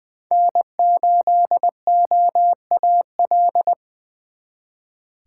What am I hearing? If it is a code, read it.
Morse code, heard as a keyed tone, about 24 wpm, 700 Hz. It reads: N8OAL
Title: Speed Racing - USA 1x3 Call Signs 20wpm